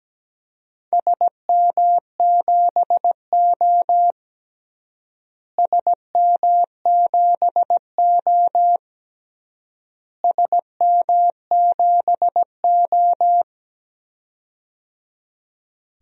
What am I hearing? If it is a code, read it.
Morse code, heard as a keyed tone, about 17 wpm, 700 Hz. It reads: SM7O SM7O SM7O
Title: DXCC-100 Call Signs - 4 Characters - Repeated 3x - 17wpm